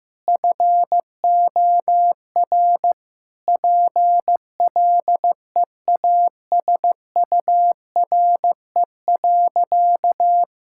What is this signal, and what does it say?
Morse code, heard as a keyed tone, about 15 wpm, 700 Hz. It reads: FOR PLEASURE.
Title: Sentences from Top 1600 Words 15wpm